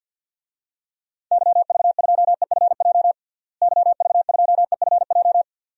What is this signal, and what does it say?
Morse code, heard as a keyed tone, about 53 wpm, 700 Hz. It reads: YV2EFJ YV2EFJ
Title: DXCC-100 Call Signs - 6 Characters - Repeated 2x - 50wpm